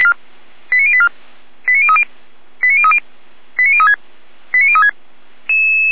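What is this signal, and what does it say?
Five-tone selective call (ZVEI), heard as quick runs of five tones, tones of about 70 ms. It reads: 89084 89030 89030 89036 89036
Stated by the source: ZVEI five-tone selective call received by SDR, from audio.f85360.feuerwehr-orlamuende.org fire brigade server.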